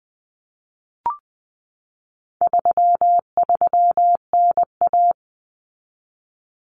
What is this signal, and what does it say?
Morse code, heard as a keyed tone, about 20 wpm, 700 Hz. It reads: 33NA
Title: Common Cut Numbers - 4-Digit 20wpm